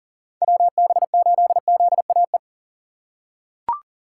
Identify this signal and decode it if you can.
Morse code, heard as a keyed tone, about 40 wpm, 700 Hz. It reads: WB8ZAE
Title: USA Call Signs - 6 Characters - 2x Character Spacing - 40wpm